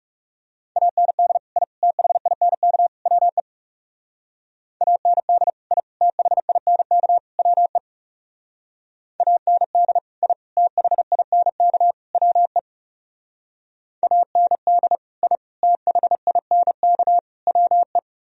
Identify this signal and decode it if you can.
Morse code, heard as a keyed tone, about 38 wpm, 700 Hz. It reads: AND I THINK WE AND I THINK WE AND I THINK WE AND I THINK WE